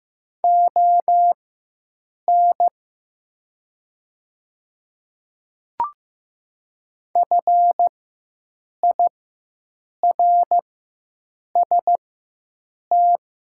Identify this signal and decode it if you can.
Morse code, heard as a keyed tone, about 15 wpm, spaced as for 7 wpm, 700 Hz. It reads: ON FIRST
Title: 4x Character Spacing - Top 100 Words - 15wpm